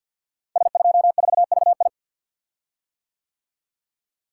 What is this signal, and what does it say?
Morse code, heard as a keyed tone, about 50 wpm, 700 Hz. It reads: S24VI
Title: Code Groups - 5 Letter:Numbers - 50wpm